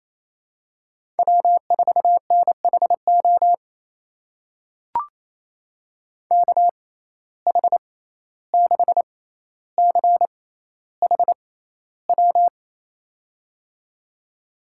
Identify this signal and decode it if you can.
Morse code, heard as a keyed tone, about 28 wpm, spaced as for 14 wpm, 700 Hz. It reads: W4NHO KH6CHW
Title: USA Call Signs - 6x Character Spacing - 28wpm